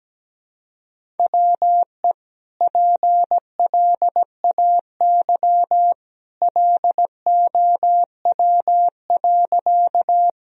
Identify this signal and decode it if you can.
Morse code, heard as a keyed tone, about 17 wpm, 700 Hz. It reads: WE PLAY LOW.
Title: Sentences from Top 1000 Words - Review Part 1 - 17wpm